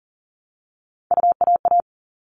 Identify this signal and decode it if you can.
Morse code, heard as a keyed tone, about 40 wpm, 700 Hz. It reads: UAA
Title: All Cut Numbers - 3-Digit 40wpm